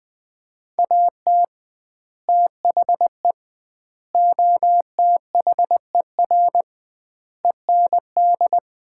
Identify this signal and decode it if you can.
Morse code, heard as a keyed tone, about 21 wpm, 700 Hz. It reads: AT THE OTHER END
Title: 2x Word-Spacing - Sets of 5 Words - 20wpm